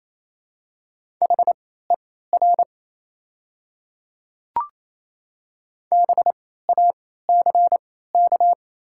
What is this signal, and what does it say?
Morse code, heard as a keyed tone, about 29 wpm, spaced as for 16 wpm, 690 Hz. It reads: HER BACK
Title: 3x Character Spacing - Top 200 Words - 28wpm